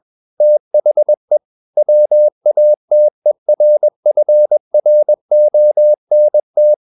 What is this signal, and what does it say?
Morse code, heard as a keyed tone, about 21 wpm, 600 Hz. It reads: THE WATERFRONT
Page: CW med Gnister